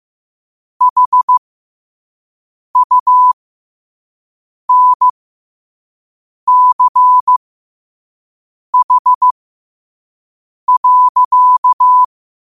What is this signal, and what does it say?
Morse code, heard as a keyed tone, about 15 wpm, spaced as for 5 wpm, 1000 Hz. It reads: HUNCH.
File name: qotdfiles / 5wpm